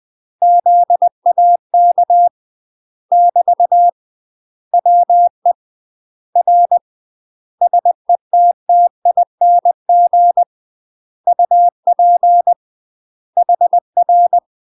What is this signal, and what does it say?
Morse code, heard as a keyed tone, about 20 wpm, 700 Hz. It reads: ZAK <BT> WE R SETTING UP HR
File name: UK MASTER QSOs TRAINING / 20WPM / UK MASTER QSOs 0024